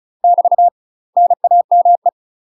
Morse code, heard as a keyed tone, about 35 wpm, 700 Hz. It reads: <BT> NAME
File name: UK MASTER QSOs TRAINING / 35WPM / UK MASTER QSOs 0037